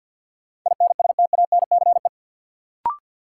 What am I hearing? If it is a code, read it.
Morse code, heard as a keyed tone, about 47 wpm, 700 Hz. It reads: INSTANCE
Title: Sentences from Top 800 Words - New Word Drill - 50wpm